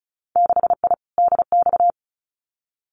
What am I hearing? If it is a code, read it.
Morse code, heard as a keyed tone, about 34 wpm, 700 Hz. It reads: 6I DX
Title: Winter Field Day Exchanges Encore 35wpm